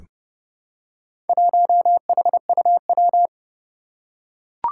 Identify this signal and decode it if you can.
Morse code, heard as a keyed tone, about 30 wpm, 700 Hz. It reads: 1HUW
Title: Four Challenging Letter-Number Combinations 30wpm